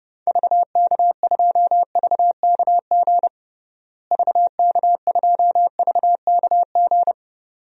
Morse code, heard as a keyed tone, about 30 wpm, 700 Hz. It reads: VK2VKG VK2VKG
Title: DXCC-100 Call Signs - 6 Characters - Repeated 2x - 30wpm